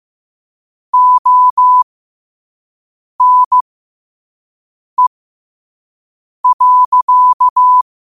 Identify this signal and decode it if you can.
Morse code, heard as a keyed tone, about 15 wpm, spaced as for 5 wpm, 1000 Hz. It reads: ONE.
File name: qotdfiles / 5wpm